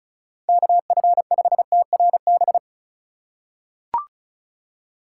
Morse code, heard as a keyed tone, about 36 wpm, 700 Hz. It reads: KF5TRB
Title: USA Call Signs - 6 Characters - 2x Character Spacing - 35wpm